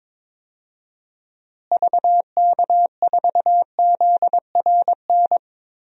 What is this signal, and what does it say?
Morse code, heard as a keyed tone, about 22 wpm, 700 Hz. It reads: VK4ZRN
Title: DXCC-100 Call Signs - 6 Characters - Repeated 2x - 22wpm